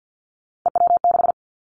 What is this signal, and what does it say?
Morse code, heard as a keyed tone, about 50 wpm, 700 Hz. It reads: EW6
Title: Callsign Prefixes for Top 100 DXCC - 50wpm